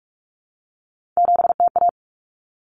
Morse code, heard as a keyed tone, about 44 wpm, 700 Hz. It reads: 7TA
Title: Common Cut Numbers - 3-Digit 45wpm